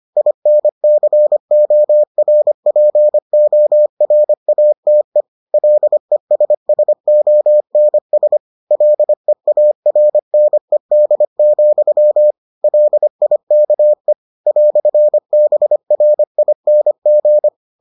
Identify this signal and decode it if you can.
Morse code, heard as a keyed tone, about 25 wpm, 600 Hz. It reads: INCORPORATE LESSONS LEARNED, LIKE "BRING